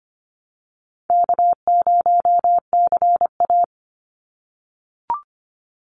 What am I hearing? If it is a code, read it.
Morse code, heard as a keyed tone, about 25 wpm, 700 Hz. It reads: K0CA
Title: Easy Call Signs - 25wpm